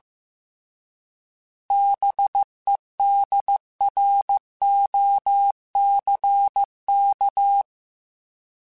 Morse code, heard as a keyed tone, about 15 wpm, 800 Hz. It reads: BEDROCK
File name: fromSMs / FromSMs mp3s / Cities@13